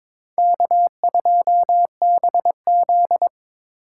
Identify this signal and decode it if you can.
Morse code, heard as a keyed tone, about 26 wpm, 700 Hz. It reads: K2BZ
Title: Speed Racing - USA Call Signs 22wpm